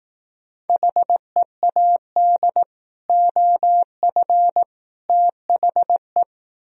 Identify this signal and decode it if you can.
Morse code, heard as a keyed tone, about 18 wpm, 700 Hz. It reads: HEAD OF THE
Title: Speed-Racing - Sets of 3 Words 15wpm